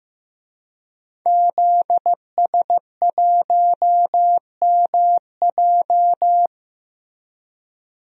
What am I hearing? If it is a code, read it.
Morse code, heard as a keyed tone, about 15 wpm, 700 Hz. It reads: ZS1MJ
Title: DXCC-100 Call Signs - 5 Characters - Repeated 2x - 15wpm